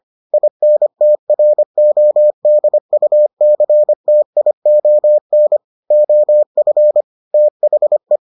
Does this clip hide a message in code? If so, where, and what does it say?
Yes: Morse code from 0.3 to 8.2 s, heard as a keyed tone, about 25 wpm, 600 Hz. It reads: INTRODUCTION OF THE